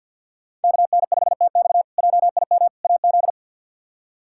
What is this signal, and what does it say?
Morse code, heard as a keyed tone, about 46 wpm, 690 Hz. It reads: KN5TX JIM AZ
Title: K1USN SST QSO Practice - 50wpm